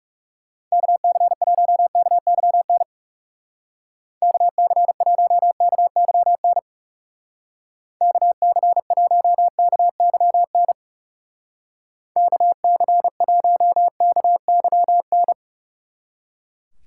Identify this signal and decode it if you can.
Morse code, heard as a keyed tone, about 39 wpm, 700 Hz. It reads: KC1KYN KC1KYN KC1KYN KC1KYN